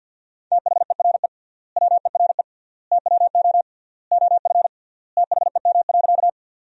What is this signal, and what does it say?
Morse code, heard as a keyed tone, about 50 wpm, 700 Hz. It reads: THERE WERE TWO OF THEM.
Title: 3x Word-Spacing - Sentences from Top 100 Words - 50wpm